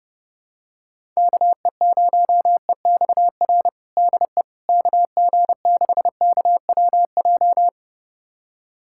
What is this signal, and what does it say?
Morse code, heard as a keyed tone, about 32 wpm, 700 Hz. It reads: KE0EXR DE KG6KWJ
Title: USA Callsign DE Callsign - 6 Characters - 30wpm